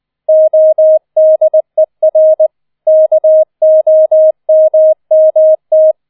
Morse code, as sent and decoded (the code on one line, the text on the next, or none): --- -.. .-. / -.- --- -- -- -
ODER KOMMT